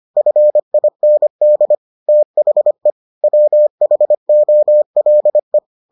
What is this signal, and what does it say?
Morse code, heard as a keyed tone, about 25 wpm, 600 Hz. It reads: FIND THE WHOLE